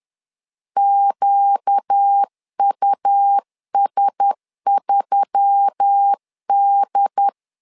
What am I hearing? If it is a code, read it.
Morse code, heard as a keyed tone, about 10 wpm, 800 Hz. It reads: QUS3D
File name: LEZIONI / 5 LBVC / 5 LBV 10wpm 2248